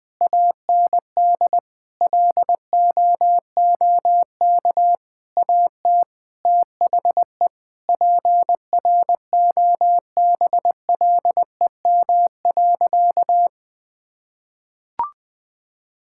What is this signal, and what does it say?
Morse code, heard as a keyed tone, about 20 wpm, 700 Hz. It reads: AND LOOK AT THE PROBLEM.
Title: Sentences with 7-Word Phrases - Spoken Beginning - 20wpm